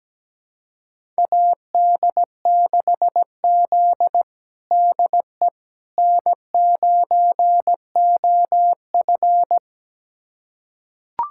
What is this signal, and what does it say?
Morse code, heard as a keyed tone, about 17 wpm, 700 Hz. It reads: AD6Z DE N9OF
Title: USA Callsign DE Callsign - 4 Characters - 17wpm